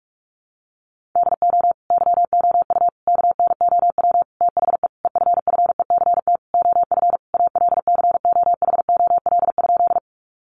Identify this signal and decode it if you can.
Morse code, heard as a keyed tone, about 44 wpm, 700 Hz. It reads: DO YOU KNOW THE EFFECT OF ALCOHOL?
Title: Sentences from Top 1400 Words 45wpm